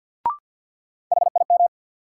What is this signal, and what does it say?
Morse code, heard as a keyed tone, about 53 wpm, 700 Hz. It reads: HIM
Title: Sentences from Top 100 Words - New Word Drill - 50wpm